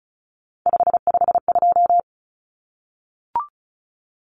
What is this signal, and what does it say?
Morse code, heard as a keyed tone, about 34 wpm, 700 Hz. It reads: 552
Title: Numbers - 3-Digit 35wpm